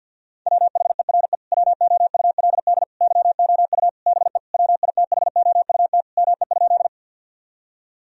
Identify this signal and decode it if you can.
Morse code, heard as a keyed tone, about 50 wpm, 700 Hz. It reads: WHERE WOULD YOU BE WITHOUT ME?
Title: Sentences from Top 1000 Words - Review Part 3 - 50wpm